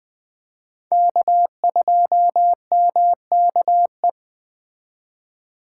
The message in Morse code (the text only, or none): K2MKE